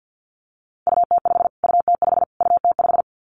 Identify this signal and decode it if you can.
Morse code, heard as a keyed tone, about 50 wpm, 700 Hz. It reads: UT5 UT5 UT5
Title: Callsign Prefixes for Top 25 DXCC - Repeated 3x - 50wpm